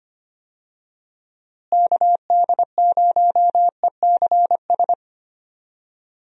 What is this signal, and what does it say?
Morse code, heard as a keyed tone, about 25 wpm, 700 Hz. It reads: KD0ECS
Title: USA Call Signs - 6 Characters - Repeated 3x - 25wpm